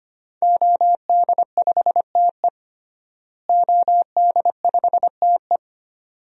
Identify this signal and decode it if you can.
Morse code, heard as a keyed tone, about 25 wpm, 700 Hz. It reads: OD5TE OD5TE
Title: DXCC-100 Call Signs - 5 Characters - Repeated 2x - 25wpm